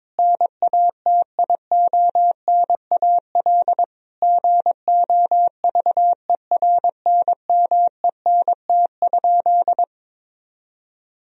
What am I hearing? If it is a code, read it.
Morse code, heard as a keyed tone, about 22 wpm, 700 Hz. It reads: NATIONAL GOVERNMENT?
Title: Sentences from Top 1000 Words - Review Part 4 - 22wpm